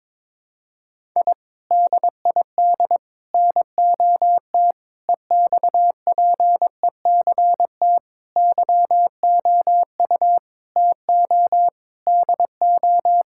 Morse code, heard as a keyed tone, about 22 wpm, 700 Hz. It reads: I DID NOT EXPECT YOU TO DO
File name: practice / 22wpm / Sentences with 3-Word Phrases - 22wpm